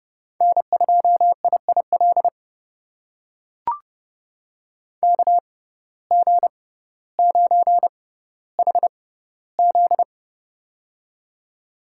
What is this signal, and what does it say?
Morse code, heard as a keyed tone, about 30 wpm, spaced as for 15 wpm, 700 Hz. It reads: N2IIL KG9HZ
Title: USA Call Signs - 5 Characters - 6x Character Spacing - 30wpm